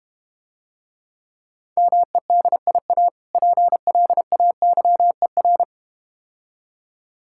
Morse code, heard as a keyed tone, about 32 wpm, 700 Hz. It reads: MEDIA PLAYER